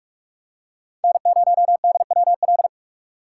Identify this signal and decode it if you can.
Morse code, heard as a keyed tone, about 41 wpm, 700 Hz. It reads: N0DWL